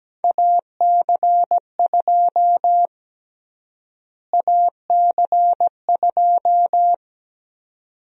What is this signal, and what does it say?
Morse code, heard as a keyed tone, about 17 wpm, 700 Hz. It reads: AC2 AC2
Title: USA Call Sign Prefixes - Repeated 2x - 17wpm